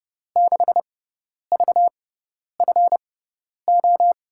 Morse code, heard as a keyed tone, about 30 wpm, spaced as for 10 wpm, 700 Hz. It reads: 6VFO
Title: USA Call Signs - 5 Characters - 6x Character Spacing - 30wpm